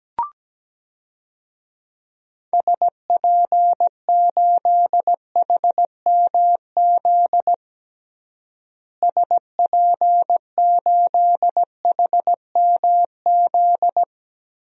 Morse code, heard as a keyed tone, about 17 wpm, 700 Hz. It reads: SP8HMZ SP8HMZ
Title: DXCC-100 Call Signs - 6 Characters - Repeated 3x - 17wpm